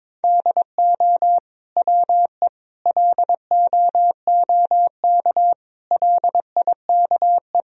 Morse code, heard as a keyed tone, about 22 wpm, 700 Hz. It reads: DO WE LOOK LIKE
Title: Sentences from Top 1000 Words - Review Part 4 - 22wpm